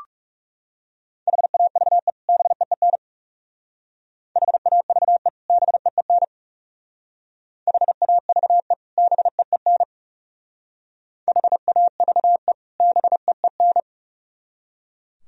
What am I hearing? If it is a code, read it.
Morse code, heard as a keyed tone, about 36 wpm, 700 Hz. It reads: HAVE BEEN HAVE BEEN HAVE BEEN HAVE BEEN
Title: Speed-Racing - Sets of 2 Words 30wpm